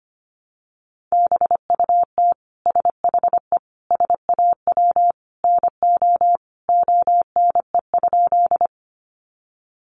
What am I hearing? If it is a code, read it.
Morse code, heard as a keyed tone, about 25 wpm, 700 Hz. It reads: BUT SHE SAW NO ONE?
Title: Sentences from Top 300 Words 25wpm